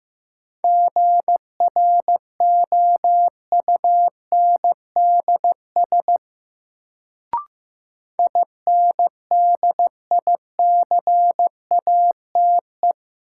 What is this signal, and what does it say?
Morse code, heard as a keyed tone, about 15 wpm, 700 Hz. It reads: GROUNDS INDICATE
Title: Sentences from Top 1300 Words - New Word Drill Alt - 15wpm